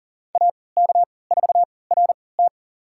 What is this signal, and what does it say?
Morse code, heard as a keyed tone, about 37 wpm, spaced as for 22 wpm, 700 Hz. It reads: AK4RT
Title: USA Call Signs - 3x Character Spacing - 40wpm